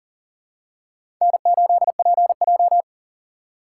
Morse code, heard as a keyed tone, about 38 wpm, 700 Hz. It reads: N8PJ
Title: USA Call Signs - 3x Character Spacing - 40wpm